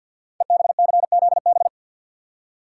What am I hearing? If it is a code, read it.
Morse code, heard as a keyed tone, about 47 wpm, 700 Hz. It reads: EBCZB
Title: Code Groups - 5 Letters - 50wpm